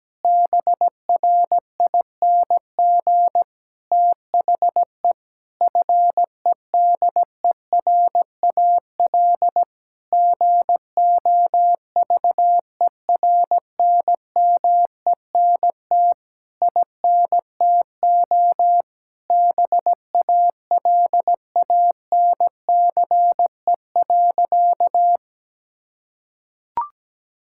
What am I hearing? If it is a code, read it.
Morse code, heard as a keyed tone, about 17 wpm, 700 Hz. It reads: BRING THE FEDERAL GOVERNMENT INTO BALANCE.
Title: Sentences from Top 1000 Words - Review Part 6 - 17wpm